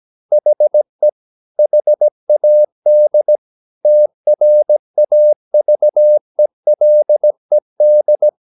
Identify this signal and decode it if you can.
Morse code, heard as a keyed tone, about 17 wpm, 600 Hz. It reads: HE HAD TRAVELED
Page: Romanen Never af Ken Follet på engelsk.